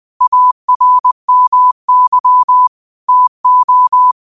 Morse code, heard as a keyed tone, about 20 wpm, 1000 Hz. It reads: ARMY TO
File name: qotdfiles / 20wpm